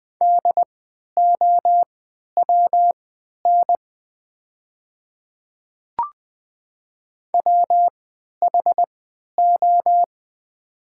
Morse code, heard as a keyed tone, about 20 wpm, spaced as for 11 wpm, 700 Hz. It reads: DOWN WHO
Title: 3x Character Spacing - Top 100 Words - 20wpm